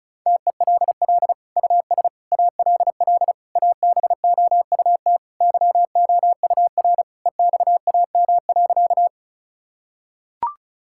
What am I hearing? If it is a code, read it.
Morse code, heard as a keyed tone, about 34 wpm, 700 Hz. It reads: TELL US ALL ABOUT YOUR EXAM.
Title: Sentences from Top 1000 Words - Review Part 5 - 35wpm